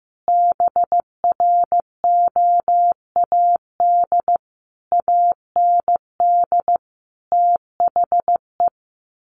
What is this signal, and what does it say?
Morse code, heard as a keyed tone, about 15 wpm, 700 Hz. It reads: BROAD AND THE
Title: A Sound Of Thunder 15wpm